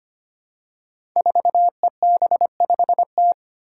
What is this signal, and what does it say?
Morse code, heard as a keyed tone, about 25 wpm, 700 Hz. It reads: 4EB5T
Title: Five Challenging Letter-Number Combinations 25wpm